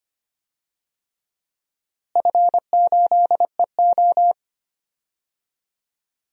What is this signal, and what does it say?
Morse code, heard as a keyed tone, about 25 wpm, 700 Hz. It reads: F8EO
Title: DXCC-100 Call Signs - 4 Characters - Repeated 3x - 25wpm